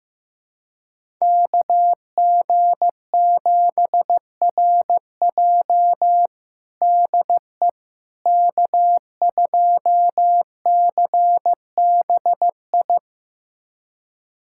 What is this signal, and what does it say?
Morse code, heard as a keyed tone, about 15 wpm, 700 Hz. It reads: KG7RJ DE K2CBI
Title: USA Callsign DE Callsign - 5 Characters - 15wpm